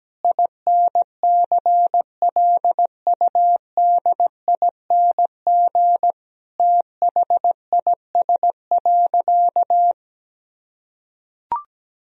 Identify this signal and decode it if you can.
Morse code, heard as a keyed tone, about 17 wpm, 700 Hz. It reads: INCLUDING THIS.
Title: Sentences from Top 1000 Words - Review Part 4 - 17wpm